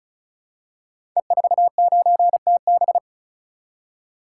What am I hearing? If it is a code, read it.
Morse code, heard as a keyed tone, about 36 wpm, 700 Hz. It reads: E49TB
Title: Code Groups - 5 Letter:Numbers - 35wpm